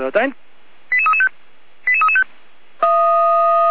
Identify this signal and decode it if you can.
Five-tone selective call (ZVEI), heard as quick runs of five tones, tones of about 70 ms, with speech. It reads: 88335 88335